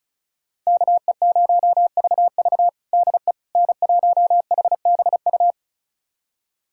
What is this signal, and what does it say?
Morse code, heard as a keyed tone, about 36 wpm, 700 Hz. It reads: KE0VV DE N1HBU
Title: USA Callsign DE Callsign - 5 Characters - 35wpm